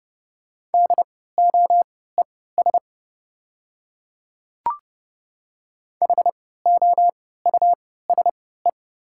Morse code, heard as a keyed tone, about 30 wpm, spaced as for 17 wpm, 690 Hz. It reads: DOES HOUSE
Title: 3x Character Spacing - Top 500 Words - 30wpm